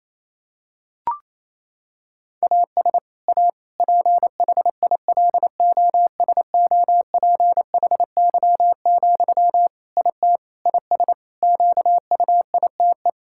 Morse code, heard as a keyed tone, about 28 wpm, 700 Hz. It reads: AS A PHILOSOPHY, IT IS QUITE
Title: Sentences from Top 1000 Words - Review Part 6 - 28wpm